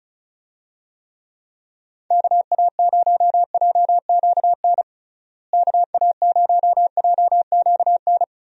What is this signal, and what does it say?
Morse code, heard as a keyed tone, about 36 wpm, 700 Hz. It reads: KA0JQN KA0JQN
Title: USA Call Signs - 6 Characters - Repeated 3x - 35wpm